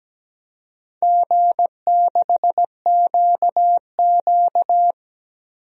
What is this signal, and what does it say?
Morse code, heard as a keyed tone, about 17 wpm, 700 Hz. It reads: G6QQ
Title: DXCC-100 Call Signs - 4 Characters - 17wpm